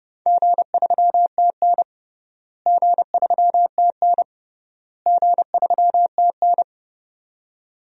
Morse code, heard as a keyed tone, about 30 wpm, 700 Hz. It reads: G3TN G3TN G3TN
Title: DXCC-100 Call Signs - 4 Characters - Repeated 3x - 30wpm